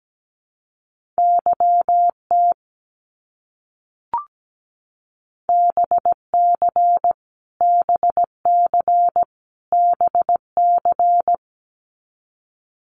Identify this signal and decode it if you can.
Morse code, heard as a keyed tone, about 17 wpm, 700 Hz. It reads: YT BC BC BC
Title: Canadian Provinces and Territories - Repeated 3x - 17wpm